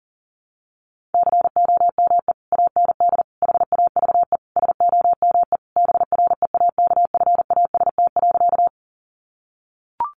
Morse code, heard as a keyed tone, about 40 wpm, 700 Hz. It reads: COME AND HAVE SOME BREAKFAST.
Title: Sentences from Top 1500 Words 40wpm